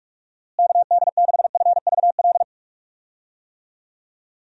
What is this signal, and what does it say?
Morse code, heard as a keyed tone, about 45 wpm, 700 Hz. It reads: KD6FVL
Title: USA Call Signs - 6 Characters - Repeated 2x - 45wpm